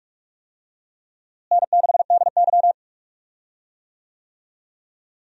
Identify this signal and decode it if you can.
Morse code, heard as a keyed tone, about 43 wpm, 700 Hz. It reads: N6DY
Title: USA Call Signs - 4 Characters - Repeated 3x - 45wpm